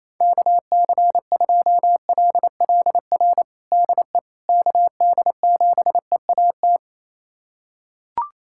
Morse code, heard as a keyed tone, about 28 wpm, 700 Hz. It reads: KC2LLR DE KD7EAT
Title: USA Callsign DE Callsign - 6 Characters - 28wpm